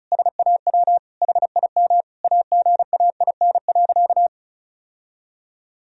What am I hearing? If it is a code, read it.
Morse code, heard as a keyed tone, about 34 wpm, 700 Hz. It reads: SAW HIM AGAIN.
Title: Sentences from Top 1000 Words - Review Part 1 - 35wpm